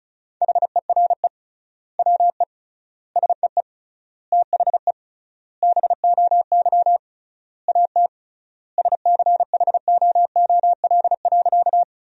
Morse code, heard as a keyed tone, about 36 wpm, 700 Hz. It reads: HERE WE SEE THE BOY AT SCHOOL.
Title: 3x Word-Spacing - Sentences from Top 300 Words - 35wpm